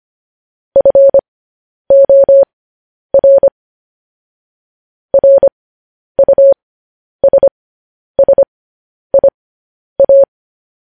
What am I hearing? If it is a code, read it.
Morse code, heard as a keyed tone, about 25 wpm, spaced as for 10 wpm, 550 Hz. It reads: FOR RUSSIA
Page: news-headlines-in-morse-code-at-10-wpm